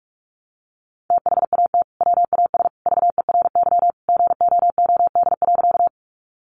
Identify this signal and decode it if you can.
Morse code, heard as a keyed tone, about 44 wpm, 700 Hz. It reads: THAT WAS VERY GOOD.
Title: Sentences from Top 200 Words 45wpm